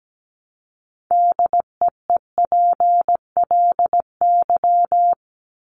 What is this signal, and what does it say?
Morse code, heard as a keyed tone, about 17 wpm, 700 Hz. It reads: DEEPLY
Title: Top 1751-2000 Words 17wpm